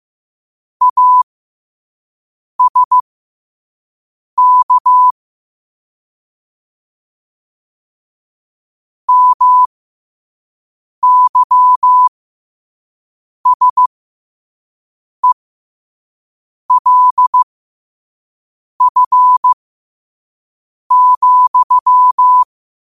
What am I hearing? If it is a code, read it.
Morse code, heard as a keyed tone, about 15 wpm, spaced as for 5 wpm, 1000 Hz. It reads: ASK MYSELF,